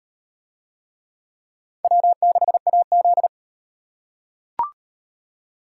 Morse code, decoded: W6AZ

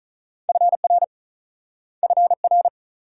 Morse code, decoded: FR FR

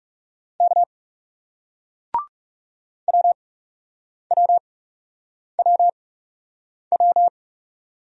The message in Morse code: K WWWW

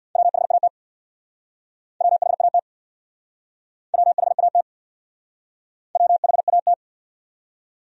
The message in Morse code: WHAT WHAT WHAT WHAT

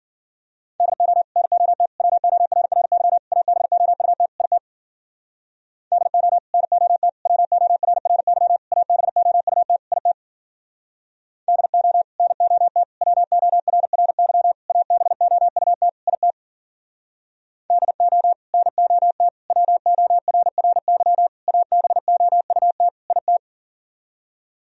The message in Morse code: DO NOT WORRY ABOUT IT DO NOT WORRY ABOUT IT DO NOT WORRY ABOUT IT DO NOT WORRY ABOUT IT

